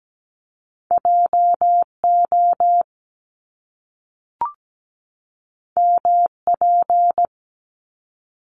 Morse code, heard as a keyed tone, about 17 wpm, 700 Hz. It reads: JO MP
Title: Top Two Letter Combinations in Words 17wpm